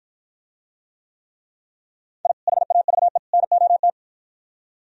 Morse code, heard as a keyed tone, about 52 wpm, 700 Hz. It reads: I HAVE NOT